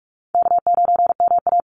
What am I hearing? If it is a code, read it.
Morse code, heard as a keyed tone, about 44 wpm, 700 Hz. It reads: K9MA
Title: Easy Call Signs Repeated 2x - 45wpm